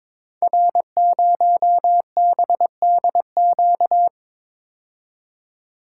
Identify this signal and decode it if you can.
Morse code, heard as a keyed tone, about 22 wpm, 700 Hz. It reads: R0BDQ
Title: Code Groups - 5 Letter:Numbers - 22wpm